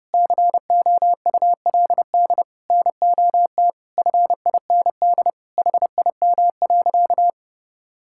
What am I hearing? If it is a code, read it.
Morse code, heard as a keyed tone, about 30 wpm, 700 Hz. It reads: COULD NOT FIND HIM.
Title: Sentences from Top 1000 Words - Review Part 5 - 30wpm